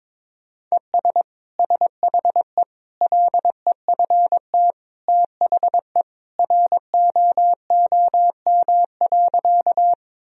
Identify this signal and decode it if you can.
Morse code, heard as a keyed tone, about 22 wpm, 700 Hz. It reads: ES SHE LEFT THE ROOM.
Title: Sentences from Top 300 Words and Abbreviations 22wpm